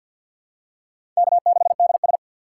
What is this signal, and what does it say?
Morse code, heard as a keyed tone, about 46 wpm, 700 Hz. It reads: K6DS
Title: USA Call Signs - 4 Characters - 3x Character Spacing - 50wpm